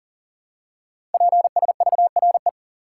Morse code, heard as a keyed tone, about 41 wpm, 690 Hz. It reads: PSVRE
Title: Code Groups - 5 Letter:Numbers - 40wpm